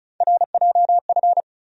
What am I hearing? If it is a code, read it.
Morse code, heard as a keyed tone, about 34 wpm, 700 Hz. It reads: RJF
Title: USA Call Sign Postfixes - 3 Letters 35wpm